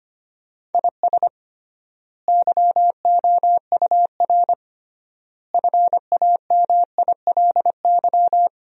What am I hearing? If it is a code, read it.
Morse code, heard as a keyed tone, about 25 wpm, 700 Hz. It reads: IS YOUR FAMILY
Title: 3x Word-Spacing - Sentences from Top 400 Words - 25wpm